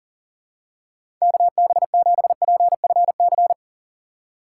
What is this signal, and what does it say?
Morse code, heard as a keyed tone, about 40 wpm, 700 Hz. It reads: KB7PFC
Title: USA Call Signs - 6 Characters - 2x Character Spacing - 40wpm